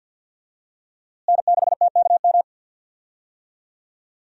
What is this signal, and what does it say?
Morse code, heard as a keyed tone, about 51 wpm, 700 Hz. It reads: N6TKM
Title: USA Call Signs - 5 Characters - Repeated 2x - 50wpm